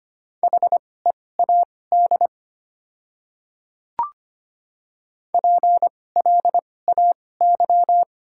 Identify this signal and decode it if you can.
Morse code, heard as a keyed tone, about 25 wpm, spaced as for 18 wpm, 700 Hz. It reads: HEAD PLAY
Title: 2x Character Spacing - Top 300 Words - 25wpm